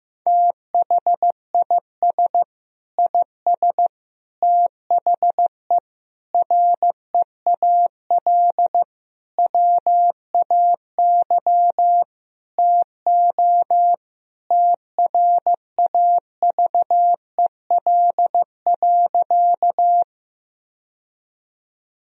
Morse code, heard as a keyed tone, about 15 wpm, 700 Hz. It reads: THIS IS THE REAL WAY TO TRAVEL.
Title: Sentences from Top 1000 Words - Review Part 2 - 15wpm